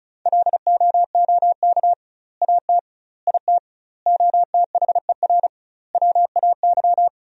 Morse code, heard as a keyed tone, about 36 wpm, 700 Hz. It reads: LOOK AT IT OTHER WAY
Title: 2x Word-Spacing - Sentences from Top 200 Words - 35wpm